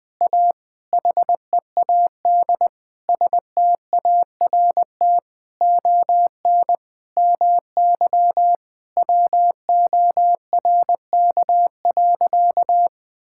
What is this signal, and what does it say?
Morse code, heard as a keyed tone, about 20 wpm, 700 Hz. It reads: A HEAD START ON MY WORK.
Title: Sentences with 7-Word Phrases - 20wpm